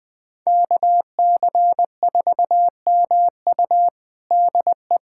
Morse code, heard as a keyed tone, about 20 wpm, 700 Hz. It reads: KC4MU DE
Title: USA Callsign DE Callsign - 5 Characters - 20wpm